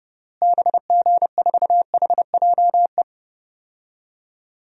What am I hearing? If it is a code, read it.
Morse code, heard as a keyed tone, about 30 wpm, 700 Hz. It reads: BG4HJE
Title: DXCC-100 Call Signs - 6 Characters - Repeated 3x - 30wpm